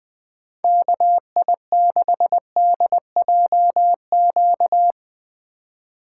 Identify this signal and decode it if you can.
Morse code, heard as a keyed tone, about 20 wpm, 700 Hz. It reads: KI6DJQ
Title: Speed Racing - USA 2x3 Call Signs 17wpm